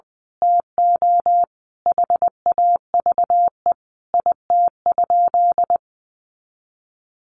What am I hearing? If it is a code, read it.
Morse code, heard as a keyed tone, about 20 wpm, 700 Hz. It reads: TO HAVE IT?
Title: Sentences from Top 100 Words 20wpm